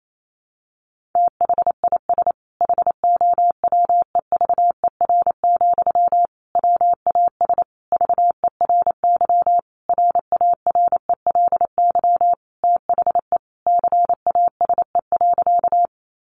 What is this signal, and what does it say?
Morse code, heard as a keyed tone, about 28 wpm, 700 Hz. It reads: THIS HOWEVER, WAS VERY RARELY THE CASE.
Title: Sentences from Top 1400 Words 28wpm